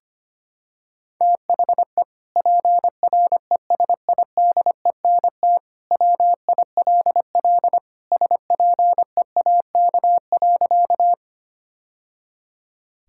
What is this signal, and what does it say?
Morse code, heard as a keyed tone, about 25 wpm, 700 Hz. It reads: THE PRESIDENT WILL SPEAK.
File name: practice / 25wpm / Sentences with 2-Word Phrases - 25wpm